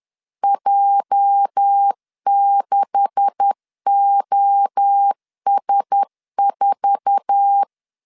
Morse code, heard as a keyed tone, about 10 wpm, 800 Hz. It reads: J6OS4
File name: CW / DET / 1 9 1500C S A 10wpm 4511